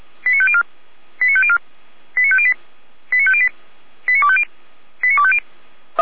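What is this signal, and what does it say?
Five-tone selective call (ZVEI), heard as quick runs of five tones, tones of about 70 ms. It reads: 89504 89504 89508 89508 89250 89250
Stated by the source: SDR-received ZVEI five-tone selective call.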